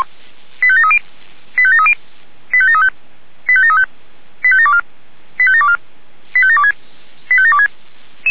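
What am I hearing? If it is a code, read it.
Five-tone selective call (ZVEI), heard as quick runs of five tones, tones of about 70 ms. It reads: 85730 85730 85735 85735 85724 85724 85726 85726
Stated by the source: ZVEI five-tone selective call received on an SDR.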